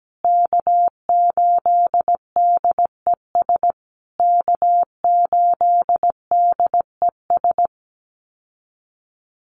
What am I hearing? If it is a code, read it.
Morse code, heard as a keyed tone, about 17 wpm, 700 Hz. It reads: K8DES K8DES
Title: USA Call Signs Repeated 3x 17wpm